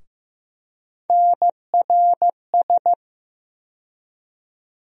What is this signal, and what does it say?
Morse code, heard as a keyed tone, about 15 wpm, 700 Hz. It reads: NRS
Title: USA Call Sign Postfixes - 3 Letters 15wpm